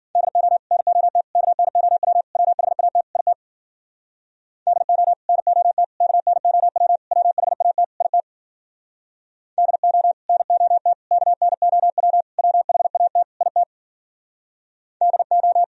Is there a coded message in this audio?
Yes: Morse code from 0.2 to 15.6 s, heard as a keyed tone, about 50 wpm, 700 Hz. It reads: DO NOT KNOW WHAT IT DO NOT KNOW WHAT IT DO NOT KNOW WHAT IT DO